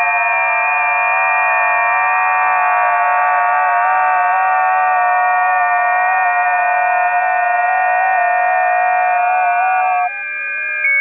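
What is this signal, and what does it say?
FT8.
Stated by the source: FT8 test recording from a busy 20 m band.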